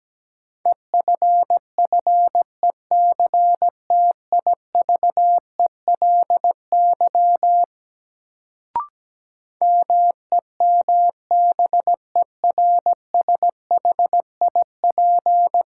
Morse code, heard as a keyed tone, about 17 wpm, 700 Hz. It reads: EFFECTIVELY MEMBERSHIP
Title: Sentences from Top 1400 Words - New Word Drill - 17wpm